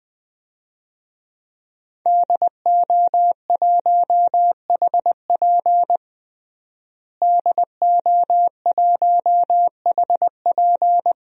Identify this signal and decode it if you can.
Morse code, heard as a keyed tone, about 20 wpm, 700 Hz. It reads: DO1HP DO1HP
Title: DXCC-100 Call Signs - 5 Characters - Repeated 2x - 20wpm